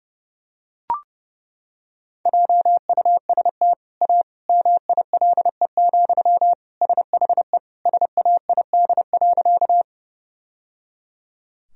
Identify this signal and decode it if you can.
Morse code, heard as a keyed tone, about 30 wpm, 700 Hz. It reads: JUST A MILE, SHE SAID.
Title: Sentences from Top 1000 Words - Review Part 1 - 30wpm